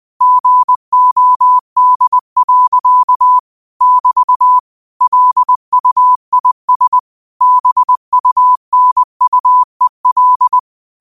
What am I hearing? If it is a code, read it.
Morse code, heard as a keyed tone, about 20 wpm, 1000 Hz. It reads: GOD. <BT> LUIS BUNUEL
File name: qotdfiles / 20wpm